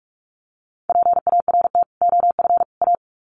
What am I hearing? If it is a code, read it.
Morse code, heard as a keyed tone, about 44 wpm, 700 Hz. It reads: PART OF A
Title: Sets of 3 Words 45wpm